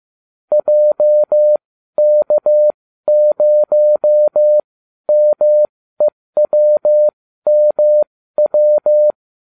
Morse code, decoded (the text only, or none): JK0MEWMW